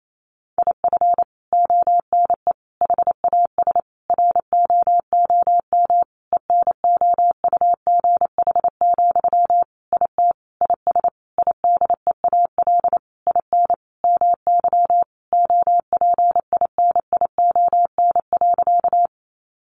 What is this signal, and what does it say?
Morse code, heard as a keyed tone, about 28 wpm, 700 Hz. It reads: IF ONE HAS ROOM ENOUGH, IT IS IDEAL IN MY OPINION.
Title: Sentences from Top 1400 Words 28wpm